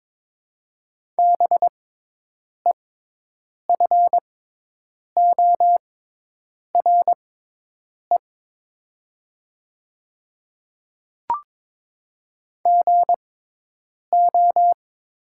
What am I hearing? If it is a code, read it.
Morse code, heard as a keyed tone, about 22 wpm, spaced as for 8 wpm, 700 Hz. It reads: BEFORE GO
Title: 6x Character Spacing - Top 200 Words - 22wpm